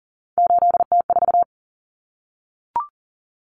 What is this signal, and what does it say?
Morse code, heard as a keyed tone, about 40 wpm, 700 Hz. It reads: 8T4
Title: Common Cut Numbers - 3-Digit 40wpm